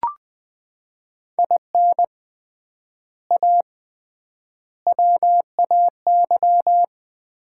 Morse code, heard as a keyed tone, about 21 wpm, 700 Hz. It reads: IN A WAY
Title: 3x Word-Spacing - Sets of 4 Words - 20wpm